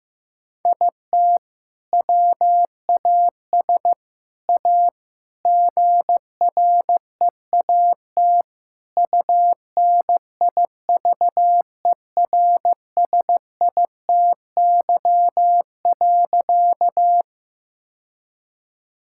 Morse code, heard as a keyed tone, about 15 wpm, 700 Hz. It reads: IT WAS A GREAT UNIVERSITY.
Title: Sentences from Top 1000 Words - Review Part 5 - 15wpm